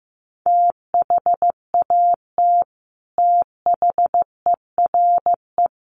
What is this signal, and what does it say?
Morse code, heard as a keyed tone, about 15 wpm, 700 Hz. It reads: THAT THERE
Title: Sets of 4 Words 15wpm